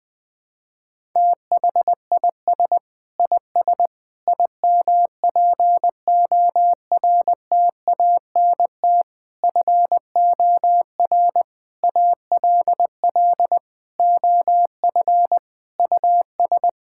Morse code, heard as a keyed tone, about 20 wpm, 700 Hz. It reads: THIS IS IMPORTANT FOR ALL OF US
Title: Sentences with 5-Word Phrases - Spoken Beginning - 20wpm